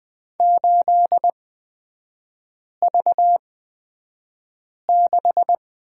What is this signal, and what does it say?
Morse code, heard as a keyed tone, about 20 wpm, spaced as for 5 wpm, 700 Hz. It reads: 8V6
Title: 236 Lesson 38e - Letter q - Three Characters 20wpm